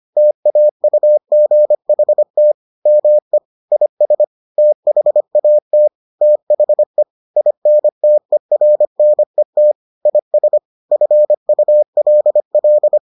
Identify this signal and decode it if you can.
Morse code, heard as a keyed tone, about 25 wpm, 600 Hz. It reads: TAUGHT ME IS THAT THE INTERNET IS FULL